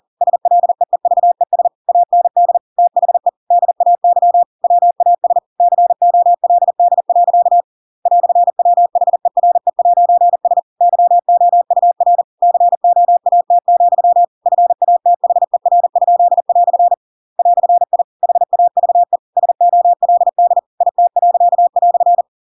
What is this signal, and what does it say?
Morse code, heard as a keyed tone, about 40 wpm, 700 Hz. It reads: SLEEVES AND THE DAY WAS COLD. "WHERE'S YOUR COAT, FATHER?" "I HAVE SOLD IT."
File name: Pinocchio / Pinocchio ch8 40 40 ik1hge